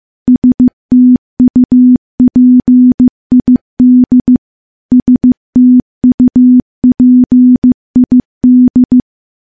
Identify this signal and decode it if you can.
Morse code, heard as a keyed tone, about 15 wpm, 250 Hz. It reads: STUPID STUPID
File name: morse-training-15WPM